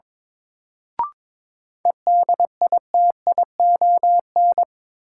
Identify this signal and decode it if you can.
Morse code, heard as a keyed tone, about 22 wpm, 700 Hz. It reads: EDITION
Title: Sets of 5 Words - New Word Drill - 22wpm